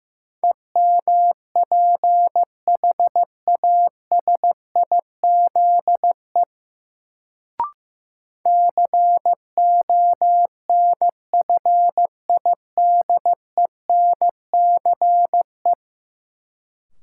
Morse code, heard as a keyed tone, about 15 wpm, 700 Hz. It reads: EMPHASIZE CONFIDENCE
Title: Sentences from Top 1500 Words - New Word Drill - 15wpm